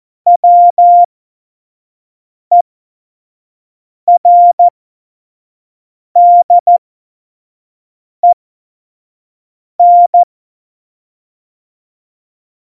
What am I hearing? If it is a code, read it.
Morse code, heard as a keyed tone, about 14 wpm, spaced as for 5 wpm, 700 Hz. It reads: WERDEN
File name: Morsekurs / Witze / Witz 29. Dezember 2020, 25 F70